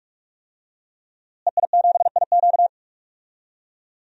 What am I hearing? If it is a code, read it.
Morse code, heard as a keyed tone, about 46 wpm, 700 Hz. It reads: EI7IQ
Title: DXCC-100 Call Signs - 5 Characters - Repeated 3x - 45wpm